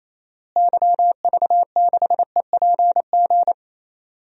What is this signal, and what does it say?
Morse code, heard as a keyed tone, about 27 wpm, 700 Hz. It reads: YV6EPG
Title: DXCC-100 Call Signs - 6 Characters - Repeated 2x - 28wpm